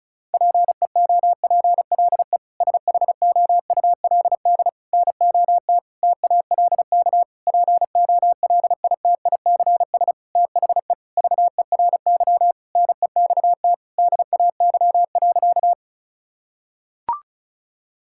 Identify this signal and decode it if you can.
Morse code, heard as a keyed tone, about 36 wpm, 700 Hz. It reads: PEOPLE SHOULD NOT TALK POLITICS THE VERY NEXT DAY.
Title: Sentences from Top 1000 Words - Review Part 6 - 35wpm